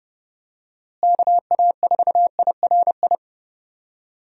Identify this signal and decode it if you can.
Morse code, heard as a keyed tone, about 32 wpm, 700 Hz. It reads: KA4IRI
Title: USA Call Signs - 6 Characters - Repeated 2x - 30wpm